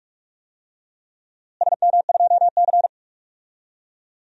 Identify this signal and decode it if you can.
Morse code, heard as a keyed tone, about 44 wpm, 700 Hz. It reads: SM2C
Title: DXCC-100 Call Signs - 4 Characters - Repeated 2x - 45wpm